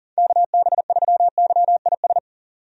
Morse code, heard as a keyed tone, about 37 wpm, 700 Hz. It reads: KB3YIS
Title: USA Call Signs - 6 Characters - 40wpm